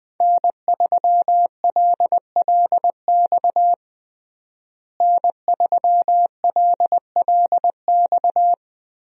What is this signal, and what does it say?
Morse code, heard as a keyed tone, about 20 wpm, 700 Hz. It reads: N3LLX N3LLX
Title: USA Call Signs - 5 Characters - Repeated 3x - 20wpm